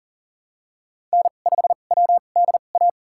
Morse code, heard as a keyed tone, about 40 wpm, spaced as for 29 wpm, 700 Hz. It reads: N5WDA
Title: USA Call Signs - 5 Characters - 2x Character Spacing - 40wpm